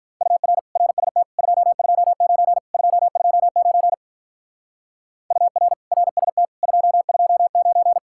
Morse code, heard as a keyed tone, about 43 wpm, 700 Hz. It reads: UR RST 229 229 UR RST 229
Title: Speed-Racing Common QSO Phrases - 35wpm